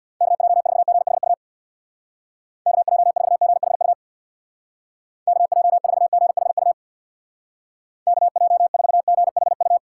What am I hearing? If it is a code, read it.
Morse code, heard as a keyed tone, about 60 wpm, 700 Hz. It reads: KJ4GHU KJ4GHU KJ4GHU KJ4GHU
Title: Speed Racing - USA 2x3 Call Signs 50wpm